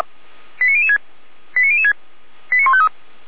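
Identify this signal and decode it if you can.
Five-tone selective call (ZVEI), heard as quick runs of five tones, tones of about 70 ms. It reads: 89006 89006 89153
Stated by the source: ZVEI five-tone selective call received by SDR.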